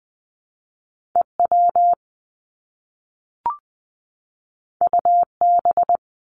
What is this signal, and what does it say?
Morse code, heard as a keyed tone, about 20 wpm, 700 Hz. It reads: EW UB
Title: Top Two Letter Combinations in Words 20wpm